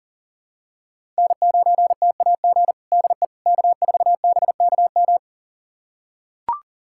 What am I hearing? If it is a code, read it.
Morse code, heard as a keyed tone, about 40 wpm, 700 Hz. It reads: N9TAG DE K4BKM
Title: USA Callsign DE Callsign - 5 Characters - 40wpm